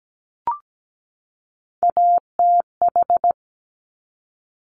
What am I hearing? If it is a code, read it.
Morse code, heard as a keyed tone, about 17 wpm, 690 Hz. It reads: ATH
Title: Top Three Letter Combinations in Words 17wpm